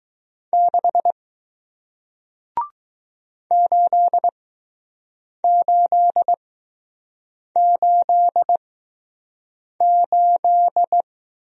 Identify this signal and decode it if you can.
Morse code, heard as a keyed tone, about 20 wpm, spaced as for 6 wpm, 700 Hz. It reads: 6 8888